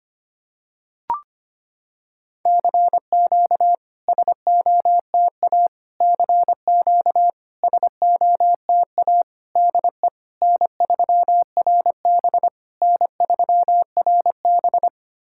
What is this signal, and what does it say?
Morse code, heard as a keyed tone, about 25 wpm, 700 Hz. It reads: CQ SOTA CQ SOTA DE N3RB N3RB
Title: SOTA QSO Practice 25wpm